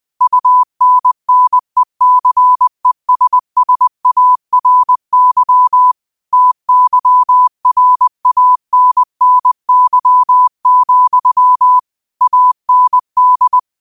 Morse code, heard as a keyed tone, about 20 wpm, 1000 Hz. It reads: UNNECESSARY TYRANNY, AND
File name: qotdfiles / 20wpm